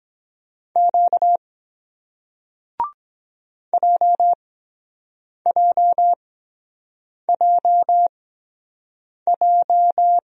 Morse code, heard as a keyed tone, about 23 wpm, spaced as for 7 wpm, 700 Hz. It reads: Q JJJJ